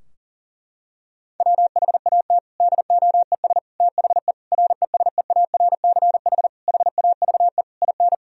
Morse code, heard as a keyed tone, about 37 wpm, 700 Hz. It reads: WHAT DOES THE RESEARCH HAVE IN
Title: Sentences from Top 1000 Words - Review Part 4 - 40wpm